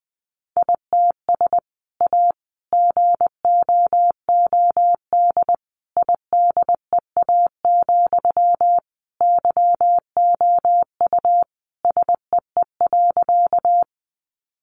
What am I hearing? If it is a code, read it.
Morse code, heard as a keyed tone, about 20 wpm, 700 Hz. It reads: ITS A GOOD IDEA, YOU SEE.
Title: Sentences from Top 600 Words 20wpm